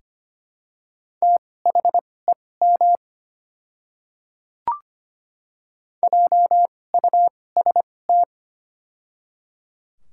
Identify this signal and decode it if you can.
Morse code, heard as a keyed tone, about 25 wpm, spaced as for 18 wpm, 700 Hz. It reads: THEM JUST
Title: 2x Character Spacing - Top 200 Words - 25wpm